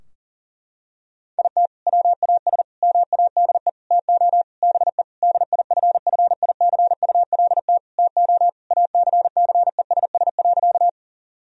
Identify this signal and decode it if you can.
Morse code, heard as a keyed tone, about 40 wpm, 700 Hz. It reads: IT WAS MADE TO BE DIFFICULT TO ACCESS.
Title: Sentences from Top 1000 Words - Review Part 4 - 40wpm